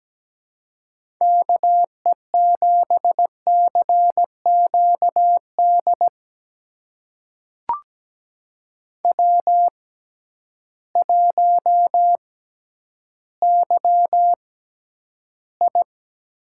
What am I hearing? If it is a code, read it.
Morse code, heard as a keyed tone, about 17 wpm, spaced as for 10 wpm, 700 Hz. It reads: KE7CQD W1YI
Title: USA Call Signs - 6x Character Spacing - 17wpm